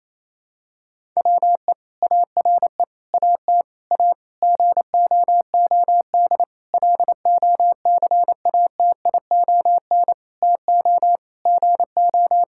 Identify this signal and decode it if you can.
Morse code, heard as a keyed tone, about 28 wpm, 700 Hz. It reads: WE ARE AT A GOOD LOCATION TO GO
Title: Sentences from Top 1000 Words - Review Part 6 - 28wpm